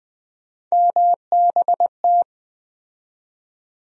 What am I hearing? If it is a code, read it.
Morse code, heard as a keyed tone, about 20 wpm, 700 Hz. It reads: MBT
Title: USA Call Sign Postfixes - 3 Letters 20wpm